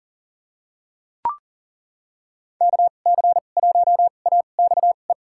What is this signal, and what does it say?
Morse code, heard as a keyed tone, about 40 wpm, spaced as for 29 wpm, 700 Hz. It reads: KC1AXE
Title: USA Call Signs - 6 Characters - 2x Character Spacing - 40wpm